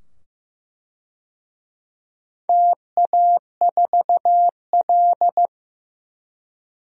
Morse code, heard as a keyed tone, about 15 wpm, 700 Hz. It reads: TA4L
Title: DXCC-100 Call Signs - 4 Characters - 15wpm